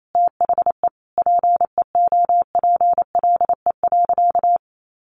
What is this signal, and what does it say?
Morse code, heard as a keyed tone, about 28 wpm, 700 Hz. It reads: THE PEOPLE.
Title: Sentences from Top 1500 Words 28wpm